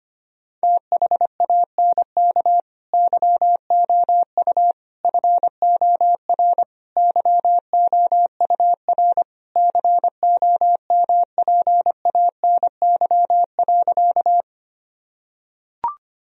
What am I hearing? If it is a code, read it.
Morse code, heard as a keyed tone, about 25 wpm, 700 Hz. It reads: THANK YOU FOR YOUR COMPANY.
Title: Sentences from Top 1000 Words - Review Part 3 - 25wpm